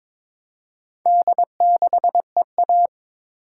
Morse code, heard as a keyed tone, about 22 wpm, 700 Hz. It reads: D6EA
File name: practice / 22wpm / Four Letter-Number Combinations - 22wpm